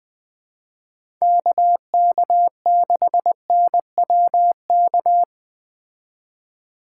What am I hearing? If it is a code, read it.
Morse code, heard as a keyed tone, about 20 wpm, 700 Hz. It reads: KK6NWK
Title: Speed Racing - USA 2x3 Call Signs 17wpm